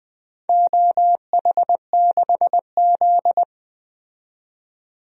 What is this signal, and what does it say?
Morse code, heard as a keyed tone, about 20 wpm, 700 Hz. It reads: OH6Z
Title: DXCC-100 Call Signs - 4 Characters - 20wpm